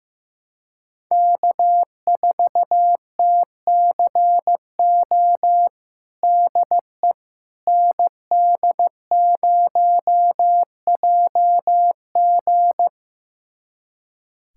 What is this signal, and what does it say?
Morse code, heard as a keyed tone, about 15 wpm, 700 Hz. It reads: K4TCO DE ND0JG
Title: USA Callsign DE Callsign - 5 Characters - 15wpm